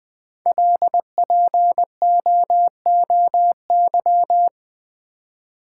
Morse code, heard as a keyed tone, about 20 wpm, 700 Hz. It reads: LPOOY
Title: Code Groups - 5 Letters - 20wpm